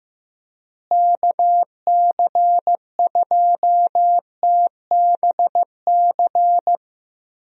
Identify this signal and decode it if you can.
Morse code, heard as a keyed tone, about 16 wpm, 700 Hz. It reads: KC2TBC
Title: USA Call Signs - 6 Characters - Repeated 3x - 15wpm